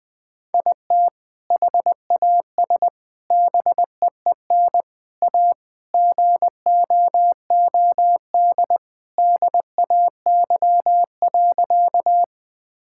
Morse code, heard as a keyed tone, about 20 wpm, 700 Hz. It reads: IT HAS BEEN A GOOD DAY.
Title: Sentences with 2-Word Phrases - Spoken Beginning - 20wpm